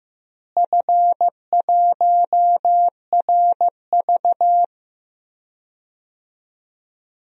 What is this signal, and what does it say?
Morse code, heard as a keyed tone, about 15 wpm, 700 Hz. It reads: F1RV
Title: DXCC-100 Call Signs - 4 Characters - Repeated 3x - 15wpm